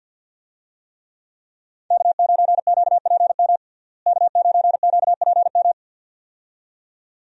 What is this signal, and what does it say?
Morse code, heard as a keyed tone, about 48 wpm, 700 Hz. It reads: K9QPM K9QPM
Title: USA Call Signs - 5 Characters - Repeated 2x - 50wpm